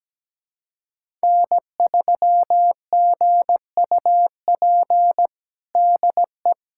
Morse code, heard as a keyed tone, about 17 wpm, 700 Hz. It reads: N3GUP DE
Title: USA Callsign DE Callsign 17wpm